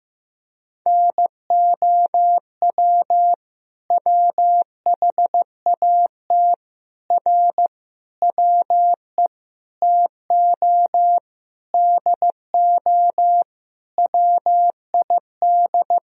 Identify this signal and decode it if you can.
Morse code, heard as a keyed tone, about 15 wpm, 700 Hz. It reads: NOW WHAT R WE TO DO WID
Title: Sentences from Top 100 Words and Abbreviations 15wpm